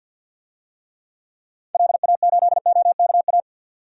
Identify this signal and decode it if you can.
Morse code, heard as a keyed tone, about 53 wpm, 700 Hz. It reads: LA8OKA
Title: DXCC-100 Call Signs - 6 Characters - 50wpm